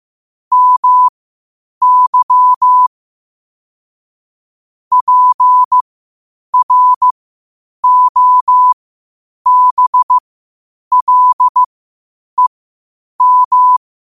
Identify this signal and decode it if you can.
Morse code, heard as a keyed tone, about 15 wpm, spaced as for 8 wpm, 1000 Hz. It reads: MY PROBLEM
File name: qotdfiles / 8wpm